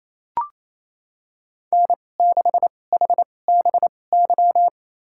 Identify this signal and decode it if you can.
Morse code, heard as a keyed tone, about 28 wpm, spaced as for 20 wpm, 700 Hz. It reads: N6HBY
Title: USA Call Signs - 2x Character Spacing - 28wpm